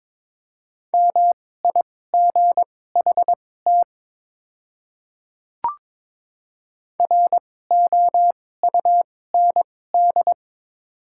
Morse code, heard as a keyed tone, about 21 wpm, spaced as for 16 wpm, 700 Hz. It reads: MIGHT ROUND
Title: 2x Character Spacing - Top 300 Words - 22wpm